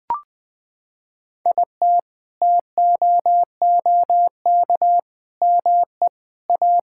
Morse code, heard as a keyed tone, about 20 wpm, 700 Hz. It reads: IT TOOK ME A
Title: Sentences with 7-Word Phrases - 20wpm